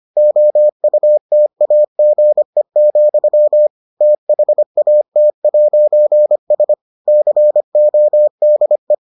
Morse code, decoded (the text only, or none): OUTAGE, THAT'S CODE